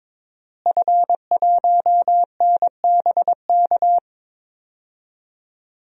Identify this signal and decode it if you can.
Morse code, heard as a keyed tone, about 22 wpm, 700 Hz. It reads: F1NBK
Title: DXCC-100 Call Signs - 5 Characters - Repeated 2x - 22wpm